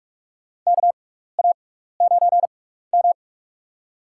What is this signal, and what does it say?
Morse code, heard as a keyed tone, about 46 wpm, spaced as for 16 wpm, 700 Hz. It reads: KA9M
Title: USA Call Signs - 4 Characters - 6x Character Spacing - 45wpm